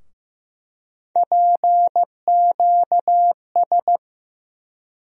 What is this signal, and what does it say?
Morse code, heard as a keyed tone, about 15 wpm, 700 Hz. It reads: PQS
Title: USA Call Sign Postfixes - 3 Letters 15wpm